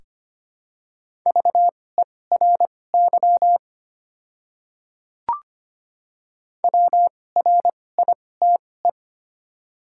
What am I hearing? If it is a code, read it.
Morse code, heard as a keyed tone, about 25 wpm, spaced as for 18 wpm, 700 Hz. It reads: VERY WRITE
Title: 2x Character Spacing - Top 200 Words - 25wpm